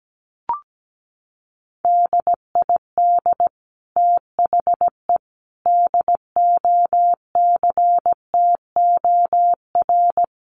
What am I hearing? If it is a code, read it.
Morse code, heard as a keyed tone, about 17 wpm, 700 Hz. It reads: DID THE DOCTOR
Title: Sentences from Top 1500 Words 17wpm